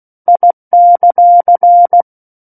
Morse code, heard as a keyed tone, about 16 wpm, 700 Hz. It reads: I;
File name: Book One / 07-Chapter 7 - 16 WPM